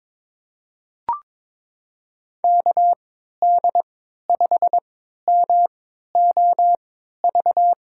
Morse code, heard as a keyed tone, about 22 wpm, spaced as for 12 wpm, 700 Hz. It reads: KD5MOV
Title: USA Call Signs - 6 Characters - 3x Character Spacing - 22wpm